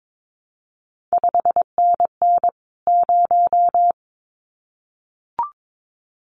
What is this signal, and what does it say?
Morse code, heard as a keyed tone, about 22 wpm, 700 Hz. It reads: 5NN 0
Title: CQ WW DX Contest Exchange Encore - 22wpm